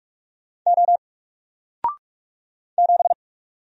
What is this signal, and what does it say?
Morse code, heard as a keyed tone, about 44 wpm, 700 Hz. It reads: O 7
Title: Single Letter-Number Alt 45wpm